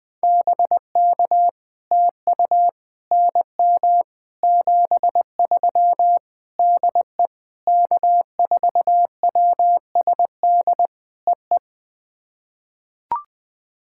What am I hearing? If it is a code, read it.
Morse code, heard as a keyed tone, about 20 wpm, 700 Hz. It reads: BK TU NM 73 DE K4WSD EE
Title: POTA QSO Practice - 20wpm